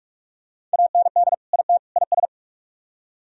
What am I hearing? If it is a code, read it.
Morse code, heard as a keyed tone, about 40 wpm, 700 Hz. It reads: AND IT IS